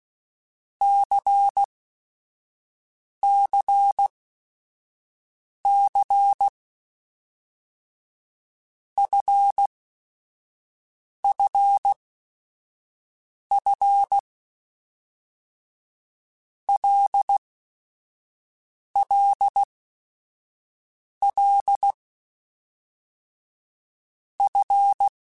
Morse code, heard as a keyed tone, about 16 wpm, 800 Hz. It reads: C C C F F F L L L F
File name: Basic Sets / Basic Letter Groups Collective7 / LFCYQ@5